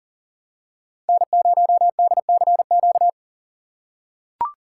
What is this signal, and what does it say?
Morse code, heard as a keyed tone, about 40 wpm, 700 Hz. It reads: N0DCQ
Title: USA Call Signs - 5 Characters - 3x Character Spacing - 40wpm